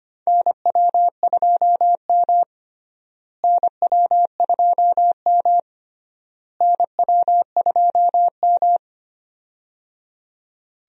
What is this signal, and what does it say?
Morse code, heard as a keyed tone, about 25 wpm, 700 Hz. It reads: NW2M NW2M NW2M
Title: USA Call Signs - 4 Characters - Repeated 3x - 25wpm